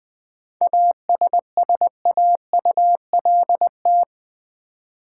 Morse code, heard as a keyed tone, about 20 wpm, 700 Hz. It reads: ASSAULT
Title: 7 Letter Words - First 3 Letters Given - 20wpm